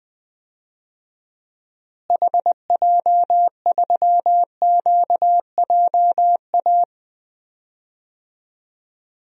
Morse code, heard as a keyed tone, about 20 wpm, 700 Hz. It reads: HJ3QJA
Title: DXCC-100 Call Signs - 6 Characters - Repeated 2x - 20wpm